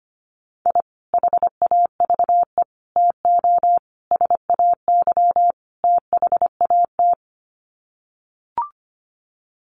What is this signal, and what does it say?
Morse code, heard as a keyed tone, about 25 wpm, 700 Hz. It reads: I HAVE TO SAY THAT
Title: Sets of 5 Words Encore 25wpm